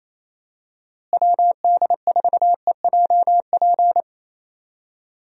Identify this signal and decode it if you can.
Morse code, heard as a keyed tone, about 28 wpm, 700 Hz. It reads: WD4EJP
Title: USA Call Signs - 6 Characters - Repeated 3x - 28wpm